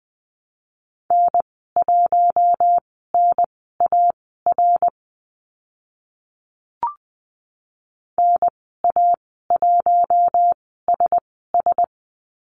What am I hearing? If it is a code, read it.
Morse code, heard as a keyed tone, about 20 wpm, spaced as for 14 wpm, 700 Hz. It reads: N1NAR NA1SS
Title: 055 Lesson 10g - Letter r - Call Signs 20wpm